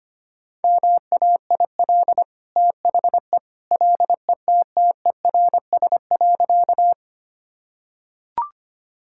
Morse code, heard as a keyed tone, about 25 wpm, 700 Hz. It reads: MAIL THE LETTERS.
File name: Easy Sentences with Hints - 25wpm